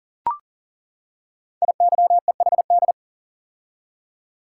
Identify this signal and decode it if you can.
Morse code, heard as a keyed tone, about 38 wpm, 690 Hz. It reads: IYEHD
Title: Code Groups - 5 Letters - 40wpm